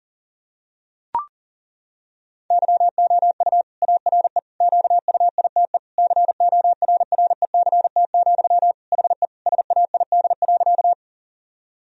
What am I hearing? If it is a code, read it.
Morse code, heard as a keyed tone, about 37 wpm, 700 Hz. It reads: YOU ARE QUITE CORRECT, HE SAID.
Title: Sentences from Top 1000 Words - Review Part 2 - 40wpm